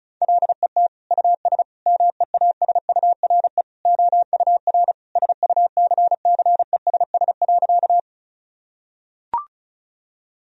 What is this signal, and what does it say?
Morse code, heard as a keyed tone, about 36 wpm, 700 Hz. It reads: LET US MEASURE OUR SUCCESS.
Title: Sentences from Top 1000 Words - Review Part 2 - 35wpm